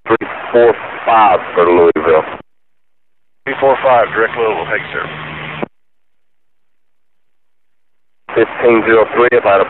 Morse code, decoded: none